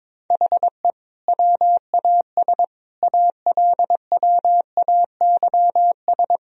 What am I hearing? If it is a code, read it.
Morse code, heard as a keyed tone, about 22 wpm, 700 Hz. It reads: HE WAS ALWAYS
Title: Sentences from Top 1000 Words - Review Part 2 - 22wpm